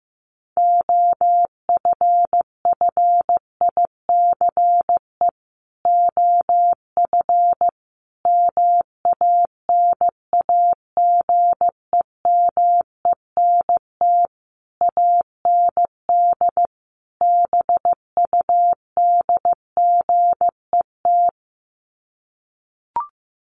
Morse code, heard as a keyed tone, about 15 wpm, 700 Hz. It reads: OFFICE OF MANAGEMENT AND BUDGET
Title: Sets of 5 Words Encore2 15wpm